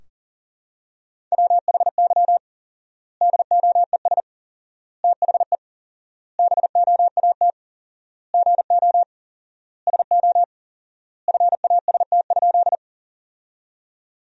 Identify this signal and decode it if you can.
Morse code, heard as a keyed tone, about 37 wpm, 700 Hz. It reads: WHY DOES THE BOAT GO SO FAST?
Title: 4x Word-Spacing - Sentences from Top 500 Words - 40wpm